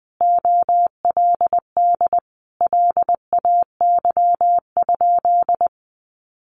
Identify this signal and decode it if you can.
Morse code, heard as a keyed tone, about 20 wpm, 700 Hz. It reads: OLD LAY?
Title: Sentences from Top 400 Words 20wpm